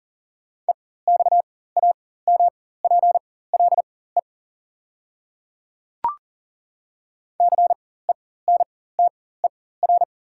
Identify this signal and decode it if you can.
Morse code, heard as a keyed tone, about 38 wpm, spaced as for 18 wpm, 690 Hz. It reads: EXAMPLE CENTER
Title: 4x Character Spacing - Top 500 Words - 40wpm